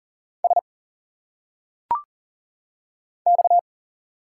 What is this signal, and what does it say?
Morse code, heard as a keyed tone, about 40 wpm, 700 Hz. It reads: S X
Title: Single Letters - Phonetic Pronunciation 40wpm